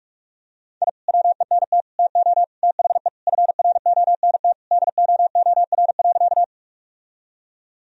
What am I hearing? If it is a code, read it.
Morse code, heard as a keyed tone, about 47 wpm, 700 Hz. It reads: I WENT TO THE FRONT DOOR.
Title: Sentences from Top 1000 Words - Review Part 2 - 45wpm